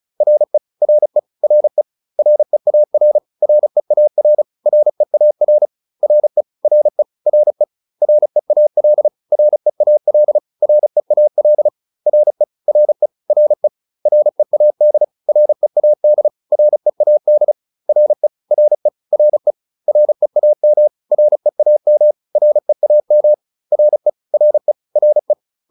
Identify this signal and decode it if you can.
Morse code, heard as a keyed tone, about 34 wpm, 600 Hz. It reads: RE RE RE REAR REAR REAR RE RE RE REAL REAL REAL RE RE RE READ READ READ RE RE RE REAM REAM REAM RE RE RE